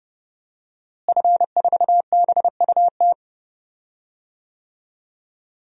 Morse code, heard as a keyed tone, about 30 wpm, 700 Hz. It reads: F4BUT
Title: DXCC-100 Call Signs - 5 Characters - Repeated 3x - 30wpm